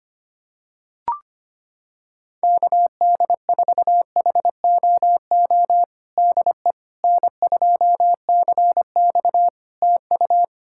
Morse code, heard as a keyed tone, about 25 wpm, 700 Hz. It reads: KD4HOO DE N2CX TU
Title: Practice QSO - Exchange 4 - Ending - 25wpm